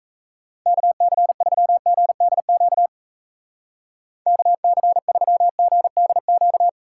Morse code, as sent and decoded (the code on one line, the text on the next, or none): -.- -.-. ...-- --. -.. --.- / -.- -.-. ...-- --. -.. --.-
KC3GDQ KC3GDQ